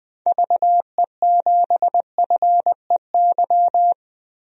Morse code, heard as a keyed tone, about 20 wpm, 700 Hz. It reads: VE7FEY
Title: DXCC-100 Call Signs - 6 Characters - Repeated 3x - 20wpm